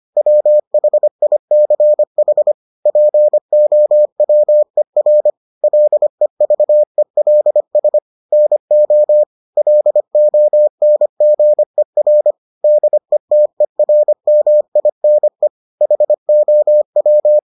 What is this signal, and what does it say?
Morse code, heard as a keyed tone, about 25 wpm, 600 Hz. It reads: WHICH POWER LEVELS NO LONGER DETERMINE HOW